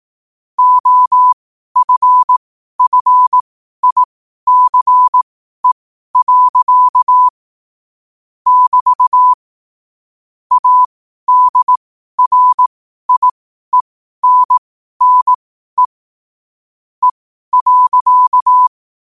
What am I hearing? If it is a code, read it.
Morse code, heard as a keyed tone, about 18 wpm, spaced as for 12 wpm, 1000 Hz. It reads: OFFICE. <BT> ADRIENNE E.
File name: qotdfiles / 12wpm